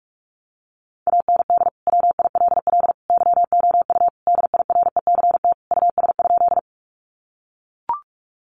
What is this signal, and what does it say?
Morse code, heard as a keyed tone, about 44 wpm, 700 Hz. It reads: AND WILL YOU DIRECT US?
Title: Sentences from Top 400 Words 45wpm